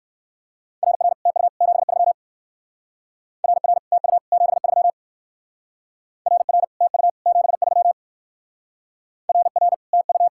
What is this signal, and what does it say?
Morse code, heard as a keyed tone, about 59 wpm, 700 Hz. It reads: RR TU 73 RR TU 73 RR TU 73 RR TU